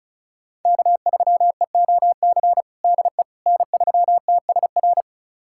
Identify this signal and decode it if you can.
Morse code, heard as a keyed tone, about 36 wpm, 700 Hz. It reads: K3EOC DE N3TSR
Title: USA Callsign DE Callsign - 5 Characters - 35wpm